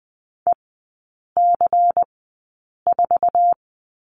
Morse code, heard as a keyed tone, about 20 wpm, spaced as for 8 wpm, 700 Hz. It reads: EC4
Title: 166 Lesson 27e - Number 7 - Three Characters 20wpm